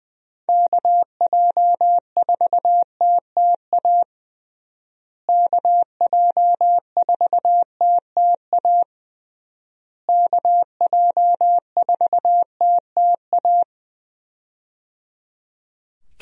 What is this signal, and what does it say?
Morse code, heard as a keyed tone, about 20 wpm, 700 Hz. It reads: KJ4TTA KJ4TTA KJ4TTA
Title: USA Call Signs - 6 Characters - Repeated 3x - 20wpm